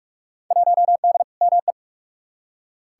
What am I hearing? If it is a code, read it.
Morse code, heard as a keyed tone, about 46 wpm, 700 Hz. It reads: 1D ME
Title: Field Day Exchanges Encore - 45wpm